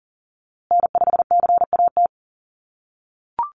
Morse code, heard as a keyed tone, about 40 wpm, 700 Hz. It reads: N5CAT
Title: USA Call Signs Repeated 2x 40wpm